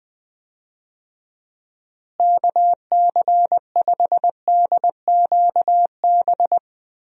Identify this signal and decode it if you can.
Morse code, heard as a keyed tone, about 20 wpm, 700 Hz. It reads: KC5DQB
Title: USA Call Signs - 6 Characters - Repeated 3x - 20wpm